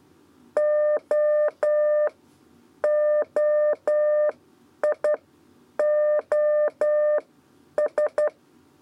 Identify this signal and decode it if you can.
Morse code, heard as a keyed tone, about 11 wpm, spaced as for 8 wpm, 600 Hz. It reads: OOIOS
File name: Lezione trasmessa 2 lez 2 exe